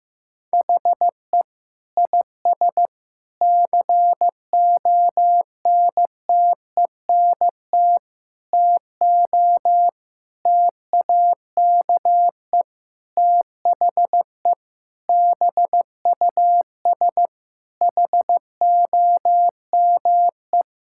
Morse code, heard as a keyed tone, about 15 wpm, 700 Hz. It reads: HE IS CONTENT TO TAKE THE BUS HOME
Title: Sentences from Top 1000 Words - Review Part 6 - 15wpm